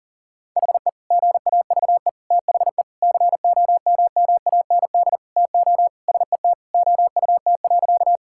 Morse code, heard as a keyed tone, about 40 wpm, 700 Hz. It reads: HE GAVE THE COMMAND TO SET OUT.
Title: Sentences from Top 1600 Words 40wpm